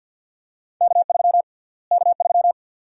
Morse code, heard as a keyed tone, about 47 wpm, 700 Hz. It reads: K3 K3
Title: USA Call Sign Prefixes - Repeated 2x - 50wpm